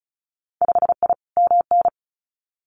Morse code, heard as a keyed tone, about 36 wpm, 700 Hz. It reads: 5I MN
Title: Winter Field Day Exchanges Encore 35wpm